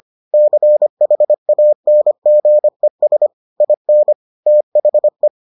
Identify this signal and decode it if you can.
Morse code, heard as a keyed tone, about 25 wpm, 600 Hz. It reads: CHANGES IN THE